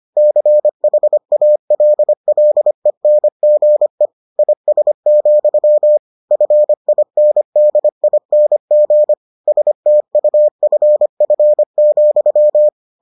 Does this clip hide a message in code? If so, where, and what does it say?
Yes: Morse code from 0.2 to 12.7 s, heard as a keyed tone, about 25 wpm, 600 Hz. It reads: CHALLENGE IS, FINDING STUFF,